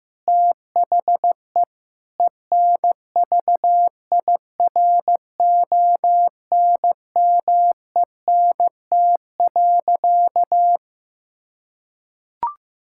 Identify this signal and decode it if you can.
Morse code, heard as a keyed tone, about 15 wpm, 700 Hz. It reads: THE ENVIRONMENT.
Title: Sentences from Top 1000 Words - Review Part 6 - 15wpm